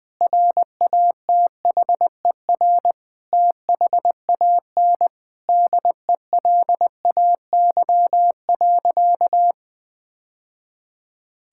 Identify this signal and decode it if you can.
Morse code, heard as a keyed tone, about 20 wpm, 700 Hz. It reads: RATHER THAN DELAY.
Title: Sentences with 2-Word Phrases - 20wpm